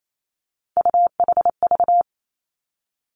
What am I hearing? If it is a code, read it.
Morse code, heard as a keyed tone, about 28 wpm, 700 Hz. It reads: UHV
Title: Three Challenging Letter-Number Combinations 28wpm